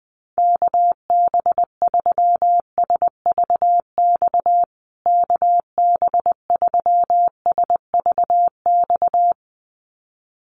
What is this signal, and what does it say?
Morse code, heard as a keyed tone, about 20 wpm, 700 Hz. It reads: KB3SVX KB3SVX
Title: USA Call Signs Repeated 2x 20wpm